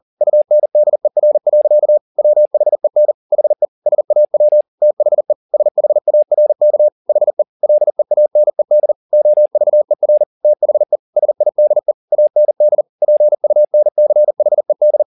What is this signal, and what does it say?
Morse code, heard as a keyed tone, about 40 wpm, 600 Hz. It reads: UNDER. WHEN HE SAW THE SHARK HE LEANED OVER THE SIDE AND PUNCHED
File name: the old man and the sea 2 40wpm